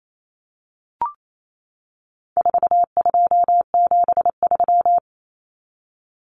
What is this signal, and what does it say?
Morse code, heard as a keyed tone, about 28 wpm, 700 Hz. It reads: 4273
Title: Numbers - 4-Digit 28wpm